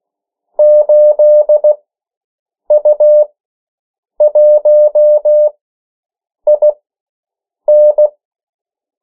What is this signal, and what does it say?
Morse code, heard as a keyed tone, about 16 wpm, spaced as for 7 wpm, 600 Hz. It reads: 8U1IN
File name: morse mp3 / 14 urok 13